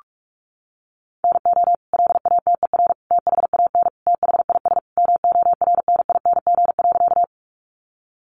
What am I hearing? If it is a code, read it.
Morse code, heard as a keyed tone, about 44 wpm, 700 Hz. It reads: NO LATER THAN THIS MORNING.
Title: Sentences from Top 700 Words 45wpm